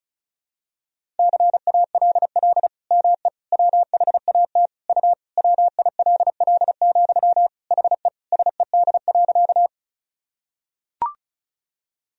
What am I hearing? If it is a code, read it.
Morse code, heard as a keyed tone, about 36 wpm, 700 Hz. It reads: CALL ME WHAT U WILL, HE SED.
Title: Sentences from Top 200 Words and Abbreviations 35wpm